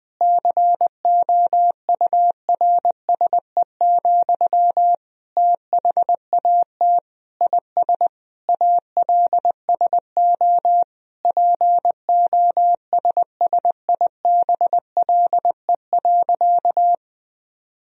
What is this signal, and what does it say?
Morse code, heard as a keyed tone, about 21 wpm, 700 Hz. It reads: COURSE, THAT IS ALSO POSSIBLE.
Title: Sentences from Top 1000 Words - Review Part 1 - 20wpm